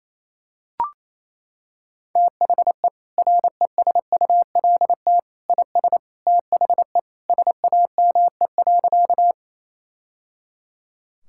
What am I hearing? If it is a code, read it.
Morse code, heard as a keyed tone, about 28 wpm, 700 Hz. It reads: THE RESULT IS THE SAME.
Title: Sentences from Top 1000 Words - Review Part 5 - 28wpm